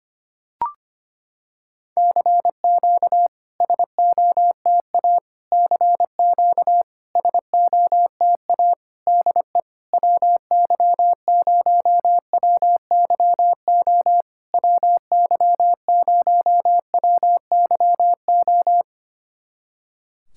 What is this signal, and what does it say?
Morse code, heard as a keyed tone, about 25 wpm, 700 Hz. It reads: CQ SOTA CQ SOTA DE WY0WYO WY0WYO
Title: SOTA QSO Practice 25wpm